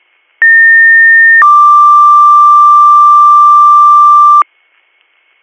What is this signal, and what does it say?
Two-tone page, heard as a one-second tone, then a three-second tone, about 1800 then 1180 Hz. It is heard from 0.4 to 4.4 s.